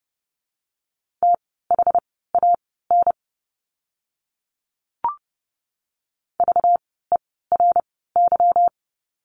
Morse code, heard as a keyed tone, about 30 wpm, spaced as for 17 wpm, 700 Hz. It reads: THAN VERY
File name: course / 30:10wpm / 237 Lesson 38f - Letter q - Words 30wpm